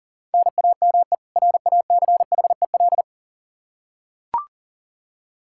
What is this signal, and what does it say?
Morse code, heard as a keyed tone, about 40 wpm, 700 Hz. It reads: NAME RACHEL
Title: Practice QSO - Exchange 1 - Beginning - 40wpm